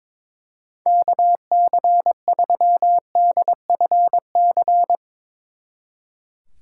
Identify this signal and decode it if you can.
Morse code, heard as a keyed tone, about 22 wpm, 700 Hz. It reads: KC3DFC
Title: Speed Racing - USA 2x3 Call Signs 22wpm